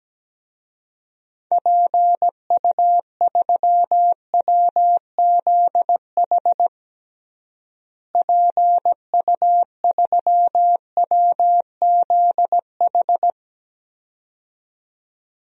Morse code, heard as a keyed tone, about 17 wpm, 700 Hz. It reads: PU3WZH PU3WZH
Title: DXCC-100 Call Signs - 6 Characters - Repeated 2x - 17wpm